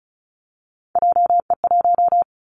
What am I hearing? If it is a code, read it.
Morse code, heard as a keyed tone, about 34 wpm, 700 Hz. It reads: JE1
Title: Callsign Prefixes for Top 25 DXCC - Repeated 3x - 35wpm